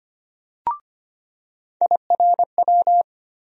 Morse code, heard as a keyed tone, about 25 wpm, 700 Hz. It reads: IRW
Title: USA Call Sign Postfixes - 3 Letters 25wpm